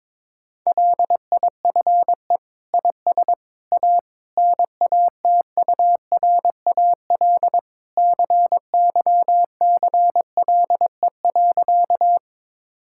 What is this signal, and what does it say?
Morse code, heard as a keyed tone, about 22 wpm, 700 Hz. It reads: LIFE IS A NATURAL CYCLE.
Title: Sentences from Top 1000 Words - Review Part 6 - 22wpm